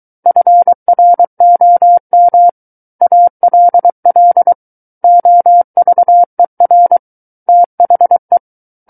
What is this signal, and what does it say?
Morse code, heard as a keyed tone, about 23 wpm, 700 Hz. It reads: FROM ALL OVER THE
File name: Book One / 14-Chapter 14 - 23 WPM